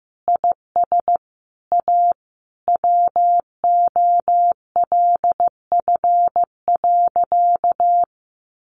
Morse code, heard as a keyed tone, about 15 wpm, 700 Hz. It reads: IS A WOLF.
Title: Additional Easy Phrases 15wpm